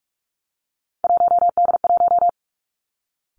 Morse code, heard as a keyed tone, about 46 wpm, 700 Hz. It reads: 1D1
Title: Three Challenging Letter-Number Combinations 45wpm